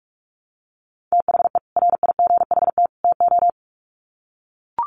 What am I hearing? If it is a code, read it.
Morse code, heard as a keyed tone, about 44 wpm, 700 Hz. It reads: THE RIGHT TO
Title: Sets of 3 Words 45wpm